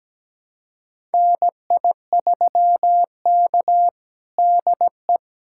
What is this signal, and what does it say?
Morse code, heard as a keyed tone, about 17 wpm, 700 Hz. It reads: NI3K DE
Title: USA Callsign DE Callsign - 4 Characters - 17wpm